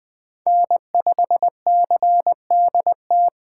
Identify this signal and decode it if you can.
Morse code, heard as a keyed tone, about 19 wpm, 700 Hz. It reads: N5CDT